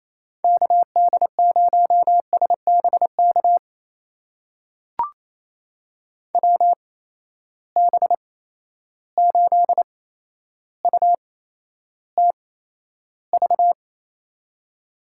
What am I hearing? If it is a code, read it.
Morse code, heard as a keyed tone, about 28 wpm, spaced as for 12 wpm, 700 Hz. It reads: KD0SBK WB8UTV
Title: USA Call Signs - 6 Characters - 8x Character Spacing - 28wpm